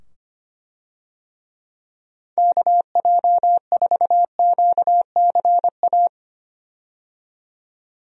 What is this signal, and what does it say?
Morse code, heard as a keyed tone, about 25 wpm, 700 Hz. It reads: KJ4QCA
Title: USA Call Signs - 6 Characters - 25wpm